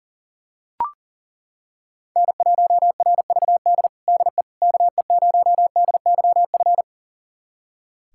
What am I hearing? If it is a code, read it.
Morse code, heard as a keyed tone, about 40 wpm, 700 Hz. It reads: N1RVD DE KE0DYF
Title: USA Callsign DE Callsign 40wpm